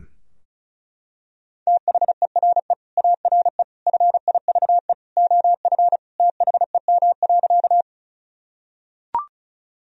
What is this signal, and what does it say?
Morse code, heard as a keyed tone, about 36 wpm, 700 Hz. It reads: THERE ARE FIVE OF THEM.
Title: Sentences from Top 1000 Words - Review Part 2 - 35wpm